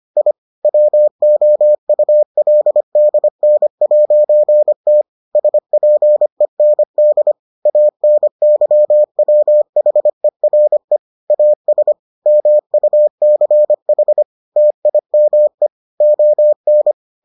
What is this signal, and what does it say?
Morse code, heard as a keyed tone, about 25 wpm, 600 Hz. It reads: I WOULDN'T SPEND ANYWHERE AS MUCH TIME ON